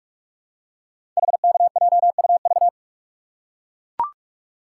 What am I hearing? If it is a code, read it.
Morse code, heard as a keyed tone, about 44 wpm, 690 Hz. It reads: HKJUV
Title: Code Groups - 5 Letters - 45wpm